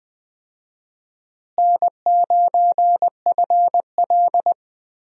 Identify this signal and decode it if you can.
Morse code, heard as a keyed tone, about 20 wpm, 700 Hz. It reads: N9FL
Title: USA Call Signs - 4 Characters - Repeated 3x - 20wpm